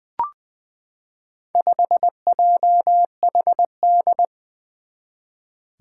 Morse code, heard as a keyed tone, about 20 wpm, 700 Hz. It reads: 5JHD
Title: Four Challenging Letter-Number Combinations 20wpm